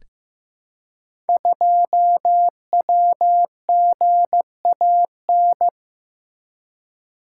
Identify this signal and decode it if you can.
Morse code, heard as a keyed tone, about 15 wpm, 700 Hz. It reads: 2WGAN
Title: Code Groups - 5 Letter:Numbers - 15wpm